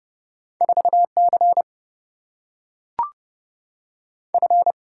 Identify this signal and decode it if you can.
Morse code, heard as a keyed tone, about 30 wpm, 700 Hz. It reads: 4 C F
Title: Single-Letter Single-Number 30wpm